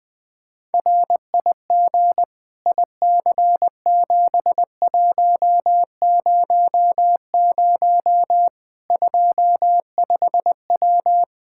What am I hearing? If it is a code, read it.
Morse code, heard as a keyed tone, about 21 wpm, 700 Hz. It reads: RIG IC7100 25W
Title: Practice QSO - Exchange 2 - Info - 20wpm